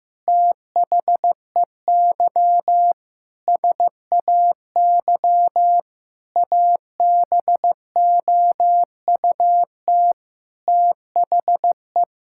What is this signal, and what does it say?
Morse code, heard as a keyed tone, about 15 wpm, 700 Hz. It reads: THEY SAY ABOUT THE
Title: Sentences from Top 1000 Words - Review Part 5 - 15wpm